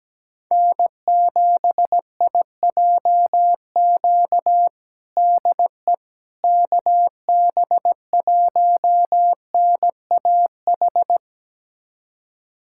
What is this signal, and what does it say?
Morse code, heard as a keyed tone, about 17 wpm, 700 Hz. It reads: N7IJQ DE KB1NAH
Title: USA Callsign DE Callsign 17wpm